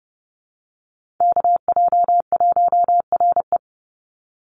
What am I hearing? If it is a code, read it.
Morse code, heard as a keyed tone, about 30 wpm, 700 Hz. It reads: KJ1RE
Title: Call Signs - 30wpm